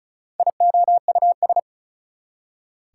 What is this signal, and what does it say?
Morse code, heard as a keyed tone, about 45 wpm, 700 Hz. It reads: IOUS